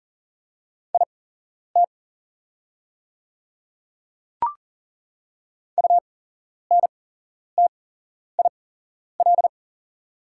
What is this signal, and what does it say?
Morse code, heard as a keyed tone, about 40 wpm, spaced as for 11 wpm, 690 Hz. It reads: IT UNTIL
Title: 8x Character Spacing - Top 300 Words - 40wpm